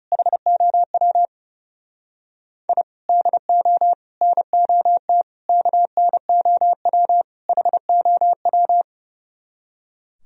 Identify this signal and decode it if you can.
Morse code, heard as a keyed tone, about 40 wpm, 700 Hz. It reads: HOW I DO NOT KNOW HOW